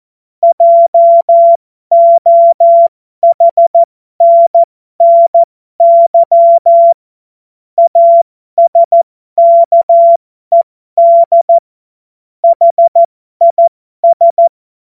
Morse code, heard as a keyed tone, about 14 wpm, spaced as for 12 wpm, 680 Hz. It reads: JOHNNY ASKED HIS
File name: Morsekurs / Witze